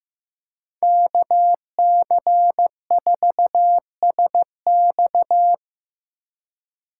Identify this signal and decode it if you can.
Morse code, heard as a keyed tone, about 18 wpm, 700 Hz. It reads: KC4SX
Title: Speed Racing - USA Call Signs 15wpm